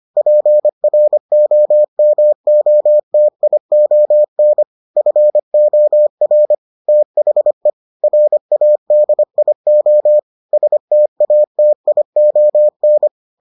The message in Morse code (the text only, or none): PROMOTION FOR THE RADIO STATION